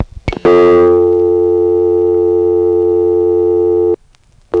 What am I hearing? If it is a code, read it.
Dial tone (North America), 350 + 440 Hz.
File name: phonetrips / HarrisonbgVA1HQ